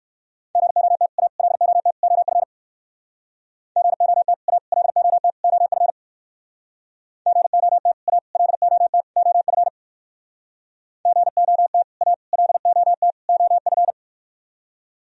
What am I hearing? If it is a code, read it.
Morse code, heard as a keyed tone, about 65 wpm, 700 Hz. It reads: GOT A LOT OF GOT A LOT OF GOT A LOT OF GOT A LOT OF